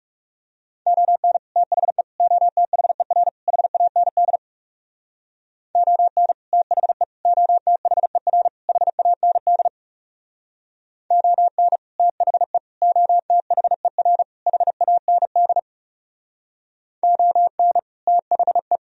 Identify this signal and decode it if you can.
Morse code, heard as a keyed tone, about 37 wpm, 700 Hz. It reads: ON THE OTHER HAND ON THE OTHER HAND ON THE OTHER HAND ON THE
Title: Speed-Racing - Sets of 4 Words 30wpm